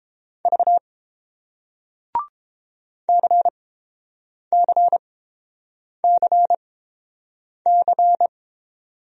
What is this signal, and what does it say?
Morse code, heard as a keyed tone, about 30 wpm, spaced as for 7 wpm, 700 Hz. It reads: V CCCC